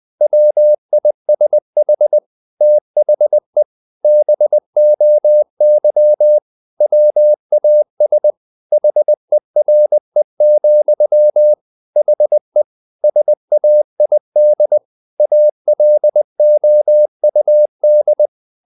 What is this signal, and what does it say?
Morse code, heard as a keyed tone, about 20 wpm, 600 Hz. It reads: WISH THE BOY WAS HERE, HE SAID ALOUD
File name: the old man and the sea / the old man and the sea part 1 20wpm